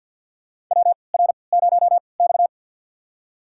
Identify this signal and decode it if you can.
Morse code, heard as a keyed tone, about 47 wpm, spaced as for 27 wpm, 700 Hz. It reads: WR0X
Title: USA Call Signs - 4 Characters - 3x Character Spacing - 50wpm